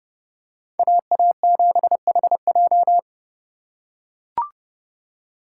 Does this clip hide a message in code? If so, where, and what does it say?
Yes: Morse code from 0.8 to 3.0 s, heard as a keyed tone, about 29 wpm, 700 Hz. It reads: AA7HJ